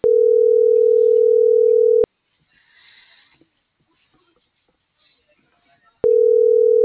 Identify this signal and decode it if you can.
Ringback tone (North America), 440 + 480 Hz.